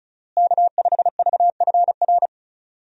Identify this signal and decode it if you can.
Morse code, heard as a keyed tone, about 36 wpm, 700 Hz. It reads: K5VFR
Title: USA Call Signs - 5 Characters - Repeated 2x - 35wpm